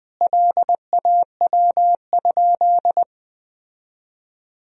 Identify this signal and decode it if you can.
Morse code, heard as a keyed tone, about 20 wpm, 700 Hz. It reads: LAW?
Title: Sentences from Top 1000 Words - Review Part 4 - 20wpm